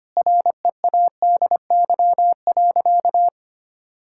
Morse code, heard as a keyed tone, about 25 wpm, 700 Hz. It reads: READY.
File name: practice / 25wpm / Sentences with 3-Word Phrases - 25wpm